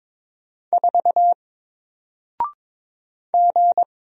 Morse code, heard as a keyed tone, about 22 wpm, 700 Hz. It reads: 4 G
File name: Single-Letter-Number-22wpm